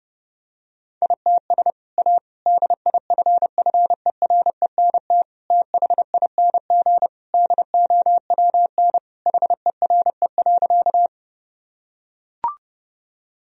Morse code, heard as a keyed tone, about 30 wpm, 700 Hz. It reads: ITS A DIFFERENT THING DOWN HERE.
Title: Sentences from Top 1000 Words - Review Part 3 - 30wpm